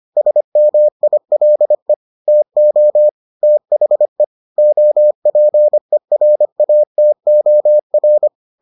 Morse code, heard as a keyed tone, about 25 wpm, 600 Hz. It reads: SMILE TO THE OPERATOR